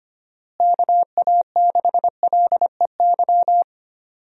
Morse code, heard as a keyed tone, about 25 wpm, 700 Hz. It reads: KA6LEY
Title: USA Call Signs - 6 Characters - Repeated 2x - 25wpm